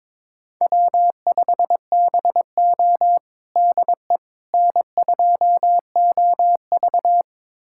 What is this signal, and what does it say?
Morse code, heard as a keyed tone, about 22 wpm, 700 Hz. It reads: W5BO DE N2OV
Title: USA Callsign DE Callsign - 4 Characters - 22wpm